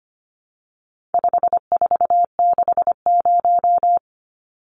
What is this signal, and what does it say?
Morse code, heard as a keyed tone, about 25 wpm, 700 Hz. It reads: 5460
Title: Numbers - 4-Digit 25wpm